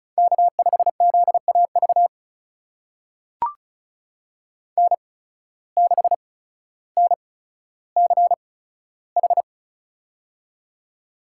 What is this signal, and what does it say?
Morse code, heard as a keyed tone, about 36 wpm, spaced as for 15 wpm, 700 Hz. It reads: K5ZAV N6NCH
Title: USA Call Signs - 5 Characters - 8x Character Spacing - 35wpm